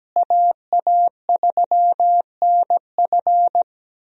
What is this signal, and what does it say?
Morse code, heard as a keyed tone, about 17 wpm, 700 Hz. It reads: AA3NF
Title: USA Call Signs - 5 Characters - Repeated 3x - 17wpm